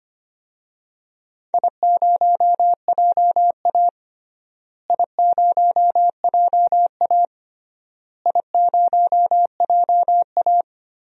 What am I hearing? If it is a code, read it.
Morse code, heard as a keyed tone, about 25 wpm, 700 Hz. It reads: I0JA I0JA I0JA
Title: DXCC-100 Call Signs - 4 Characters - Repeated 3x - 25wpm